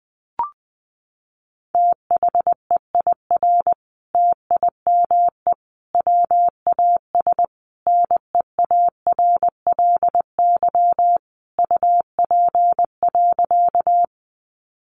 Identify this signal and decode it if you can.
Morse code, heard as a keyed tone, about 20 wpm, 700 Hz. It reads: THEIR TIME WAS NEARLY UP.
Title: Sentences from Top 1300 Words 20wpm